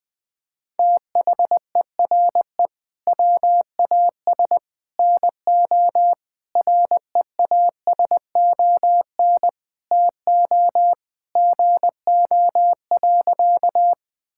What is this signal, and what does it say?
Morse code, heard as a keyed tone, about 20 wpm, 700 Hz. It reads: THERE WAS NO REASON TO GO.
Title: Sentences with 3-Word Phrases - Spoken Beginning - 20wpm